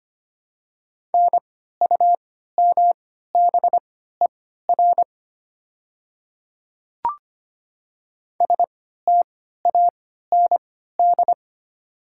Morse code, heard as a keyed tone, about 25 wpm, spaced as for 14 wpm, 700 Hz. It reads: NUMBER STAND
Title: 3x Character Spacing - Top 300 Words - 25wpm